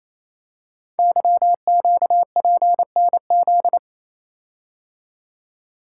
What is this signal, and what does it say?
Morse code, heard as a keyed tone, about 27 wpm, 700 Hz. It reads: YQPNZ
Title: Code Groups - 5 Letters - 28wpm